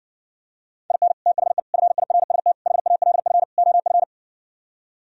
Morse code, heard as a keyed tone, about 69 wpm, 700 Hz. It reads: IN THE FIRST HALF OF